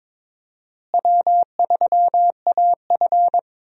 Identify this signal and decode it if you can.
Morse code, heard as a keyed tone, about 21 wpm, 700 Hz. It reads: W3AF
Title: USA Call Signs - 4 Characters - Repeated 2x - 22wpm